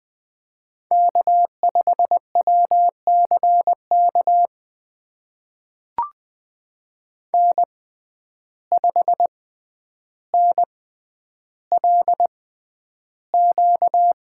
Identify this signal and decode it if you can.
Morse code, heard as a keyed tone, about 20 wpm, spaced as for 10 wpm, 700 Hz. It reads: K5WCK N5NLQ
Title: USA Call Signs - 5 Characters - 6x Character Spacing - 20wpm